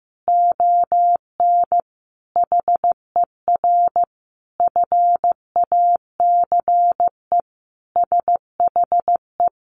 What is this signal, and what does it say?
Morse code, heard as a keyed tone, about 15 wpm, 700 Hz. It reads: ON HER FACE SHE
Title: Sentences from Top 1200 Words 15wpm